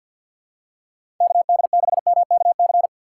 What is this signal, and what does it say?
Morse code, heard as a keyed tone, about 53 wpm, 700 Hz. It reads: KD6MKC
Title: USA Call Signs - 6 Characters - 6x Character Spacing - 50wpm